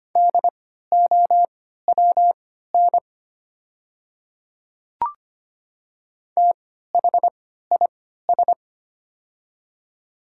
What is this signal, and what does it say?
Morse code, heard as a keyed tone, about 25 wpm, spaced as for 14 wpm, 700 Hz. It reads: DOWN THIS
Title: 3x Character Spacing - Top 200 Words - 25wpm